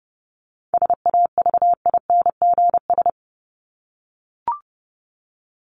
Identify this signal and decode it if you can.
Morse code, heard as a keyed tone, about 30 wpm, 700 Hz. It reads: SAVINGS
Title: Top 1001-1250 Words 30wpm